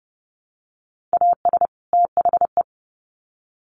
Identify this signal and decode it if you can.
Morse code, heard as a keyed tone, about 30 wpm, 700 Hz. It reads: AS THE
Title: Sets of 2 Words 30wpm